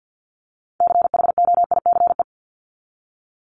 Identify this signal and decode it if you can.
Morse code, heard as a keyed tone, about 50 wpm, 700 Hz. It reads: CHOICE
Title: Top 751-1000 Words 50wpm